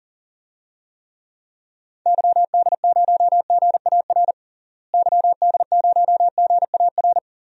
Morse code, heard as a keyed tone, about 40 wpm, 700 Hz. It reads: YD0GAR YD0GAR
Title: DXCC-100 Call Signs - 6 Characters - Repeated 2x - 40wpm